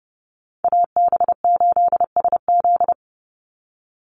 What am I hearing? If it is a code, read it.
Morse code, heard as a keyed tone, about 30 wpm, 700 Hz. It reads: AB8SZ
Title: USA Call Signs 30wpm